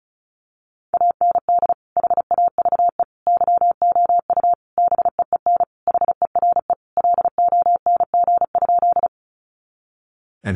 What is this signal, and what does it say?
Morse code, heard as a keyed tone, about 34 wpm, 700 Hz. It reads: AND HAVE YOU BEEN HERE LONG?
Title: Sentences from Top 200 Words 35wpm